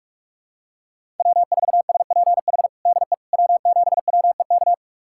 Morse code, heard as a keyed tone, about 46 wpm, 700 Hz. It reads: W4SPH DE W7WEK
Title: USA Callsign DE Callsign - 5 Characters - 45wpm